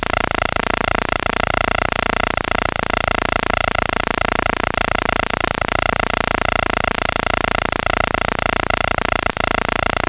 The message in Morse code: none